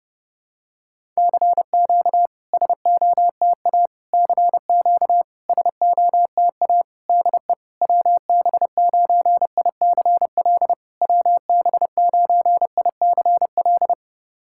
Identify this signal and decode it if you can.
Morse code, heard as a keyed tone, about 32 wpm, 690 Hz. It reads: CQ SOTA CQ SOTA DE WB9ICL WB9ICL
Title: SOTA QSO Practice 30wpm